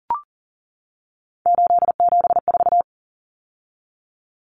Morse code, heard as a keyed tone, about 40 wpm, 700 Hz. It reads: 874
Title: Numbers - 3-Digit 40wpm